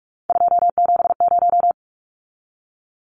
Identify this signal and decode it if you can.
Morse code, heard as a keyed tone, about 44 wpm, 700 Hz. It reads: 270
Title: Numbers - 3-Digit 45wpm